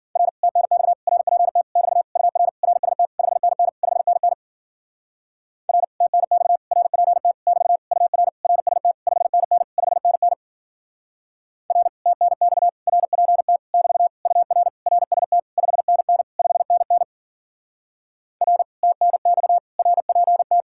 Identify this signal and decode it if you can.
Morse code, heard as a keyed tone, about 49 wpm, 700 Hz. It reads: R TNX RPT <BT> UR RST 5NN 5NN R TNX RPT <BT> UR RST 5NN 5NN R TNX RPT <BT> UR RST 5NN 5NN R TNX RPT
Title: Speed-Racing Common QSO Phrases Encore - 40wpm